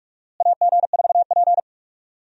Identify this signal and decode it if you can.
Morse code, heard as a keyed tone, about 46 wpm, 700 Hz. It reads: AG4P
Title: USA Call Signs - 4 Characters - Repeated 3x - 45wpm